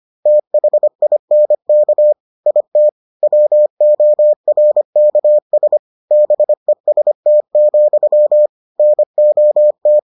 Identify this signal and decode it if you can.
Morse code, heard as a keyed tone, about 25 wpm, 600 Hz. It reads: THINK IT WORKS BEST, NOT